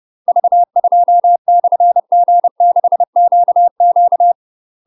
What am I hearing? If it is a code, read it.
Morse code, heard as a keyed tone, about 30 wpm, 700 Hz. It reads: V2/G6QQ
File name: UK MASTER QSOs TRAINING / 30WPM / UK MASTER QSOs 0024